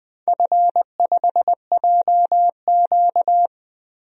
Morse code, heard as a keyed tone, about 20 wpm, 700 Hz. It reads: F5JQ
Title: DXCC-100 Call Signs - 4 Characters - 20wpm